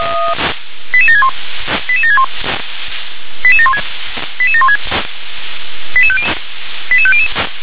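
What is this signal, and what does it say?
Five-tone selective call (ZVEI), heard as quick runs of five tones, tones of about 70 ms. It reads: 88761 88761 88716 88716 88500
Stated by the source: ZVEI five-tone selective call 88793 received on SDR.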